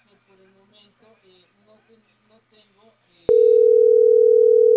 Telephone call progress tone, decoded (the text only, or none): ringback tone